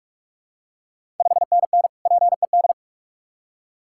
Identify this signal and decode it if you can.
Morse code, heard as a keyed tone, about 41 wpm, 690 Hz. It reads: 5NN PED
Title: Ontario QSO Party Exchange 45wpm